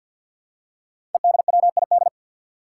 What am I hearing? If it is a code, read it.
Morse code, heard as a keyed tone, about 53 wpm, 700 Hz. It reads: EDWID
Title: Five Challenging Letter-Number Combinations 50wpm